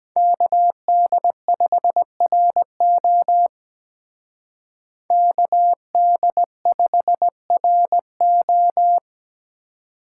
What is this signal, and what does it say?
Morse code, heard as a keyed tone, about 20 wpm, 700 Hz. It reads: KD5RO KD5RO
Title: Speed Racing - USA 2x2 Call Signs 17wpm